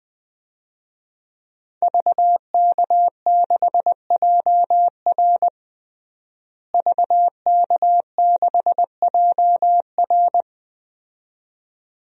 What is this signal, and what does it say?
Morse code, heard as a keyed tone, about 20 wpm, 700 Hz. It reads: VK6JR VK6JR
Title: DXCC-100 Call Signs - 5 Characters - Repeated 2x - 20wpm